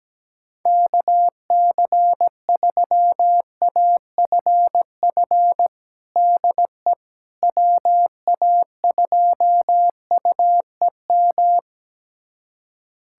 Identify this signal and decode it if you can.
Morse code, heard as a keyed tone, about 17 wpm, 700 Hz. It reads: KC3AFF DE WA2UEM
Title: USA Callsign DE Callsign - 6 Characters - 17wpm